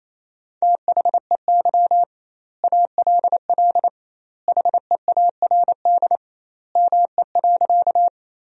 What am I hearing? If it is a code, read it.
Morse code, heard as a keyed tone, about 28 wpm, 700 Hz. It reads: THEY ALL HEARD ME.
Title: 2x Word-Spacing - Sentences from Top 400 Words - 28wpm